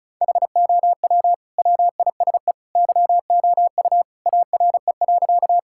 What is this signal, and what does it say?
Morse code, heard as a keyed tone, about 36 wpm, 700 Hz. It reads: HOW WISE YOU ARE.
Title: Sentences from Top 1600 Words 35wpm